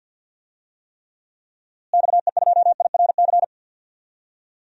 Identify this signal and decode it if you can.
Morse code, heard as a keyed tone, about 52 wpm, 700 Hz. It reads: XE2IRC
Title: DXCC-100 Call Signs - 6 Characters - Repeated 3x - 50wpm